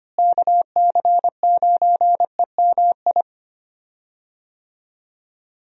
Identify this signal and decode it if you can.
Morse code, heard as a keyed tone, about 25 wpm, 700 Hz. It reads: KC9EMI
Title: USA Call Signs - 6 Characters - Repeated 2x - 25wpm